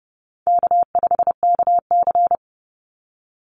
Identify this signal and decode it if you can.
Morse code, heard as a keyed tone, about 30 wpm, 700 Hz. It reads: K5KC
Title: Call Signs - 30wpm